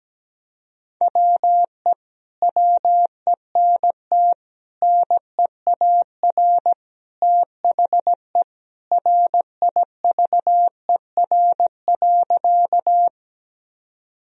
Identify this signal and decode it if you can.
Morse code, heard as a keyed tone, about 17 wpm, 700 Hz. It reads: WE WENT NEAR THE RIVER.
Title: Sentences from Top 1000 Words - Review Part 1 - 17wpm